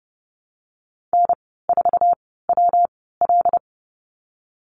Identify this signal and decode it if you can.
Morse code, heard as a keyed tone, about 30 wpm, 700 Hz. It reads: N 4 W L
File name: Lesson 19g - Letter w - Call Signs 30wpm